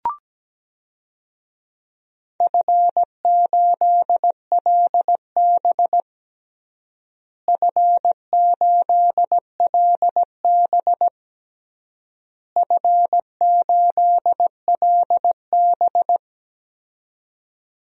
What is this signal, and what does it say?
Morse code, heard as a keyed tone, about 17 wpm, 700 Hz. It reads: F8LB F8LB F8LB
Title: DXCC-100 Call Signs - 4 Characters - Repeated 3x - 17wpm